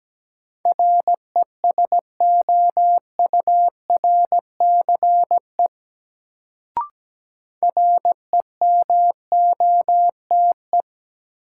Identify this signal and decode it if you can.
Morse code, heard as a keyed tone, about 17 wpm, 700 Hz. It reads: RESOURCE REMOTE
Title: Sentences from Top 1300 Words - New Word Drill - 17wpm